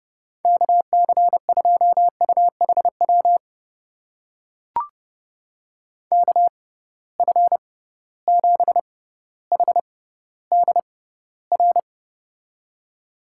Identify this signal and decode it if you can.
Morse code, heard as a keyed tone, about 31 wpm, spaced as for 16 wpm, 700 Hz. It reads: KC2UHW KF7HDR
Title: USA Call Signs - 6 Characters - 6x Character Spacing - 30wpm